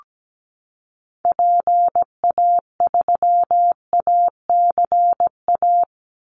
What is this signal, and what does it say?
Morse code, heard as a keyed tone, about 17 wpm, 700 Hz. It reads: PA3ACA
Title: Call Signs Repeated 3x - 17wpm